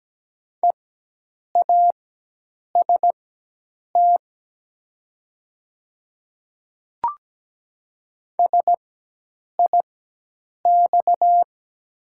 Morse code, heard as a keyed tone, about 17 wpm, spaced as for 8 wpm, 700 Hz. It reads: EAST SIX
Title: 4x Character Spacing - Top 500 Words - 17wpm